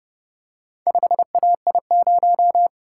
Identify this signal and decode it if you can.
Morse code, heard as a keyed tone, about 29 wpm, 700 Hz. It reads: 5AI0